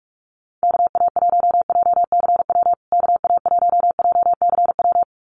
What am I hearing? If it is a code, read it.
Morse code, heard as a keyed tone, about 44 wpm, 690 Hz. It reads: KA1JCW KA1JCW
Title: USA Call Signs Repeated 2x 45wpm